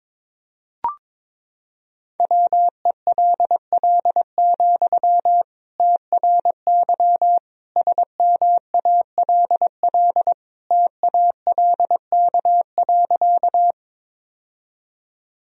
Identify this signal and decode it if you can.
Morse code, heard as a keyed tone, about 22 wpm, 700 Hz. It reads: WELL, TRY SMALL TALK.
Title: Sentences from Top 1000 Words - Review Part 1 - 22wpm